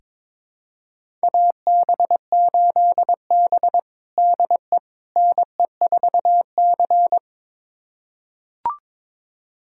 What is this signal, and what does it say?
Morse code, heard as a keyed tone, about 22 wpm, 700 Hz. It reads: AB8B DE NE4C
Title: USA Callsign DE Callsign - 4 Characters - 22wpm